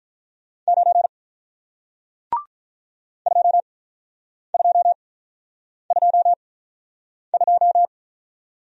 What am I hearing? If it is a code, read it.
Morse code, heard as a keyed tone, about 48 wpm, spaced as for 9 wpm, 700 Hz. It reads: 9 2222